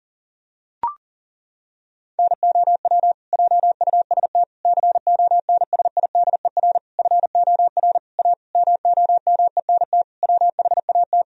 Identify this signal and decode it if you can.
Morse code, heard as a keyed tone, about 40 wpm, 700 Hz. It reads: NOW JUST CONSIDER FOR A MOMENT WHAT
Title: Sentences from Top 1000 Words - Review Part 4 - 40wpm